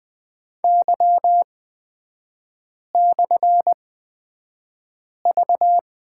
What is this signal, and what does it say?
Morse code, heard as a keyed tone, about 20 wpm, spaced as for 5 wpm, 700 Hz. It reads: Y/V
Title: 255 Lesson 41e - Period - Three Characters 20wpm